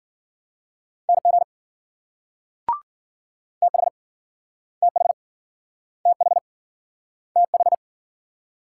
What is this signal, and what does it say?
Morse code, heard as a keyed tone, about 51 wpm, 700 Hz. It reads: NG TH TH TH TH